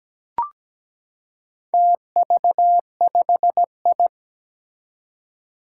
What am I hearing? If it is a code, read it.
Morse code, heard as a keyed tone, about 17 wpm, 700 Hz. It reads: TV5I
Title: Four Challenging Letter-Number Combinations 17wpm